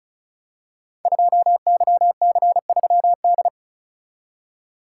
Morse code, heard as a keyed tone, about 36 wpm, 700 Hz. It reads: 2YC3D
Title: Code Groups - 5 Letter:Numbers - 35wpm